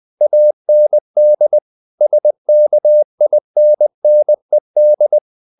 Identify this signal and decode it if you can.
Morse code, heard as a keyed tone, about 20 wpm, 600 Hz. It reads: AND SKINNED
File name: the old man and the sea part 2 20wpm